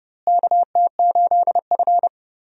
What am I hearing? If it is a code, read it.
Morse code, heard as a keyed tone, about 32 wpm, 700 Hz. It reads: KT8F
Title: USA Call Signs - 4 Characters - Repeated 3x - 30wpm